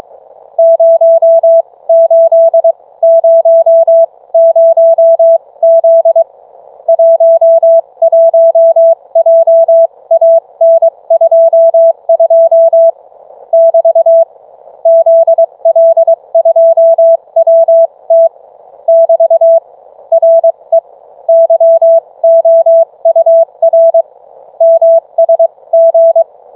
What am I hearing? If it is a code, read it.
Morse code, heard as a keyed tone, about 22 wpm, spaced as for 19 wpm, 660 Hz. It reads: 0800Z 11JAN22 <BT> ZL2WT <BT> RE YOUR MSG